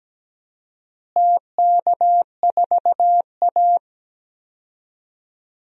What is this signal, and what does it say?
Morse code, heard as a keyed tone, about 17 wpm, 700 Hz. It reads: TK4A
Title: DXCC-100 Call Signs - 4 Characters - Repeated 2x - 17wpm